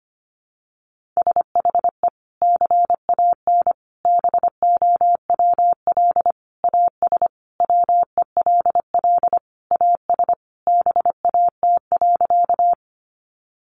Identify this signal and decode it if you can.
Morse code, heard as a keyed tone, about 25 wpm, 700 Hz. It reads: SHE CAN BOWL AS WELL AS BAT.
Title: Sentences from Top 1500 Words 25wpm